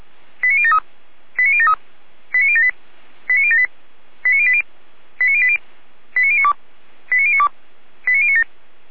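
Five-tone selective call (ZVEI), heard as quick runs of five tones, tones of about 70 ms. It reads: 89073 89073 89078 89078 89080 89080 89092 89092 89097
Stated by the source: ZVEI five-tone selective call 89067 received via SDR, archived by Feuerwehr Orlamünde (fire brigade).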